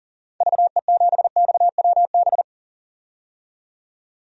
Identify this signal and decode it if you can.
Morse code, heard as a keyed tone, about 37 wpm, 700 Hz. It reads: VE7XWB
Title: DXCC-100 Call Signs - 6 Characters - 40wpm